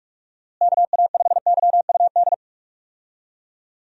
Morse code, heard as a keyed tone, about 46 wpm, 700 Hz. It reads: KA5YUD
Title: USA Call Signs - 6 Characters - Repeated 2x - 45wpm